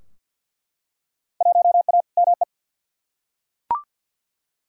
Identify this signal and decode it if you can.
Morse code, heard as a keyed tone, about 50 wpm, 700 Hz. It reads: 1A ME
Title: Field Day Exchanges Encore - 50wpm